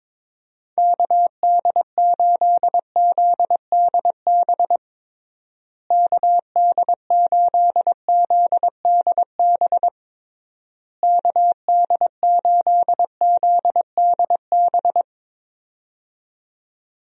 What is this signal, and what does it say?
Morse code, heard as a keyed tone, about 22 wpm, 700 Hz. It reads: KD8ZDB KD8ZDB KD8ZDB
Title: USA Call Signs - 6 Characters - Repeated 3x - 22wpm